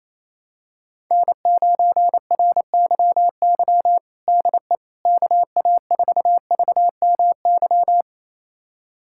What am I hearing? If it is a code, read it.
Morse code, heard as a keyed tone, about 28 wpm, 700 Hz. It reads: N9RYY DE KA4VMY
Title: USA Callsign DE Callsign 28wpm